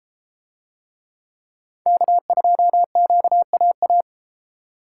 Morse code, heard as a keyed tone, about 31 wpm, 700 Hz. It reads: K2QAA